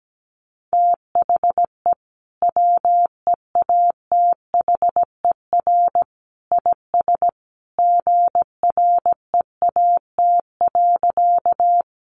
Morse code, heard as a keyed tone, about 17 wpm, 700 Hz. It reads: THE WEATHER IS GREAT.
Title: Sentences from Top 800 Words 17wpm